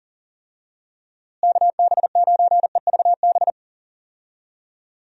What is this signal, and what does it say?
Morse code, heard as a keyed tone, about 40 wpm, 700 Hz. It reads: KB9EVB
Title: USA Call Signs - 6 Characters - Repeated 3x - 40wpm